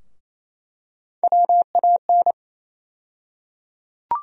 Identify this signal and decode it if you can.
Morse code, heard as a keyed tone, about 28 wpm, 700 Hz. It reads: WAN
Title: USA Call Sign Postfixes - 3 Letters 28wpm